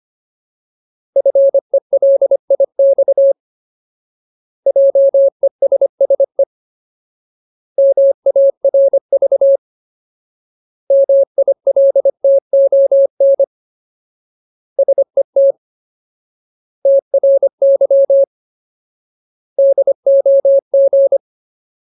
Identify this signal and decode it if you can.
Morse code, heard as a keyed tone, about 25 wpm, 550 Hz. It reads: FELIX JESSE MARV MILTON SET TRY DOG
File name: Ultimate Word List / 25WPM Ultimate